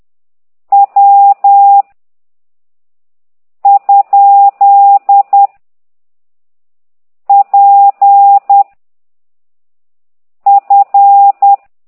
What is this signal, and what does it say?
Morse code, heard as a keyed tone, about 10 wpm, 800 Hz. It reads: W ? P F
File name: ct2idw / exerc9